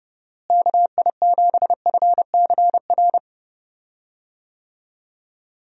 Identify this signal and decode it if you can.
Morse code, heard as a keyed tone, about 30 wpm, 700 Hz. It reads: KI7FCR
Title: USA Call Signs - 6 Characters - Repeated 2x - 30wpm